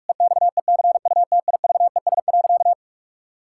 Morse code, heard as a keyed tone, about 48 wpm, 700 Hz. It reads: EXECUTIVES.
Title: Pangram Challenge v2 - 45wpm